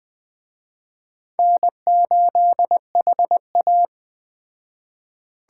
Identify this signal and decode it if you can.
Morse code, heard as a keyed tone, about 21 wpm, 700 Hz. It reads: N8HA